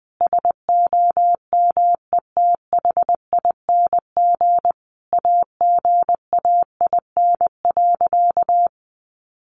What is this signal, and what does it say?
Morse code, heard as a keyed tone, about 20 wpm, 700 Hz. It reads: SOMETHING AGAIN.
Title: Sentences from Top 1100 Words 20wpm